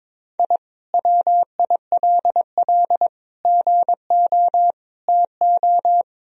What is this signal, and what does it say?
Morse code, heard as a keyed tone, about 21 wpm, 700 Hz. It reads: I WILL GO TO
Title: Sentences from Top 1000 Words - Review Part 4 - 22wpm